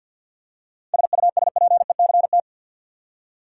Morse code, heard as a keyed tone, about 51 wpm, 700 Hz. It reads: SUSPECT
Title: Seven Letter Words - 50wpm